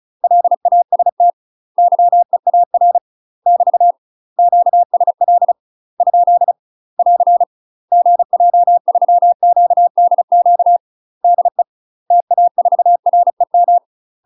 Morse code, heard as a keyed tone, about 34 wpm, 700 Hz. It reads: LAST YEAR <BT> QSL ? <AR> GJ3QDQ DE TA4REM
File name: UK MASTER QSOs TRAINING/35WPM